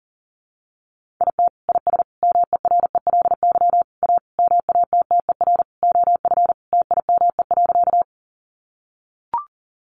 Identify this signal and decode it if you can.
Morse code, heard as a keyed tone, about 40 wpm, 700 Hz. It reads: IT IS MERELY A MATTER OF TIME.
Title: Sentences from Top 1400 Words 40wpm